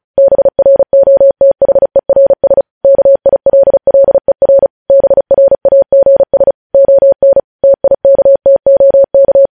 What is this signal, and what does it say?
Morse code, heard as a keyed tone, about 36 wpm, 550 Hz. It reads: BROTHERS KILLER BRAGS ON TIKTOK